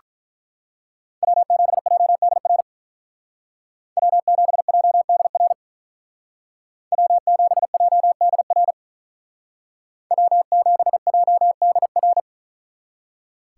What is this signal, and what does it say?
Morse code, heard as a keyed tone, about 42 wpm, 700 Hz. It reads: W7JDR W7JDR W7JDR W7JDR
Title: Speed Racing - USA 1x3 Call Signs 35wpm